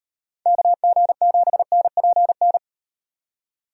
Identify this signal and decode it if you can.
Morse code, heard as a keyed tone, about 38 wpm, 700 Hz. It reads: KG7NPN